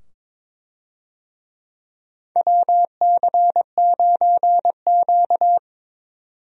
Morse code, heard as a keyed tone, about 22 wpm, 700 Hz. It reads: WC9Q
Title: USA Call Signs - 4 Characters - 22wpm